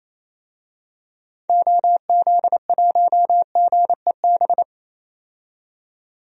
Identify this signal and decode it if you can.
Morse code, heard as a keyed tone, about 27 wpm, 700 Hz. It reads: OZ1GEB
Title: DXCC-100 Call Signs - 6 Characters - 28wpm